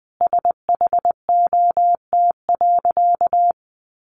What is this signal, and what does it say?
Morse code, heard as a keyed tone, about 20 wpm, 700 Hz. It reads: SHOT.
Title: Sentences from Top 1100 Words 20wpm